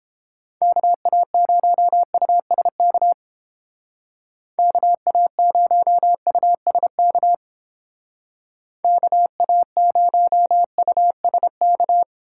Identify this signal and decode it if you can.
Morse code, heard as a keyed tone, about 28 wpm, 700 Hz. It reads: KA0USK KA0USK KA0USK